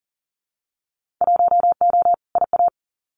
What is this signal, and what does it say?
Morse code, heard as a keyed tone, about 40 wpm, 700 Hz. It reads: 1O IA
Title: Winter Field Day Exchanges 40wpm